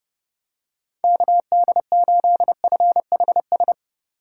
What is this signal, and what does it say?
Morse code, heard as a keyed tone, about 31 wpm, 700 Hz. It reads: KD8FHS